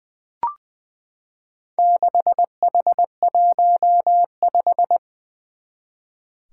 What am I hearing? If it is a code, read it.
Morse code, heard as a keyed tone, about 20 wpm, 700 Hz. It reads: 6H15
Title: Four Challenging Letter-Number Combinations 20wpm